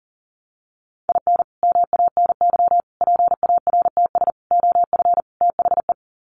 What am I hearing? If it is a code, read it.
Morse code, heard as a keyed tone, about 40 wpm, 700 Hz. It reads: IN MANY PARTS OF THE
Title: Sets of 5 Words Encore 40wpm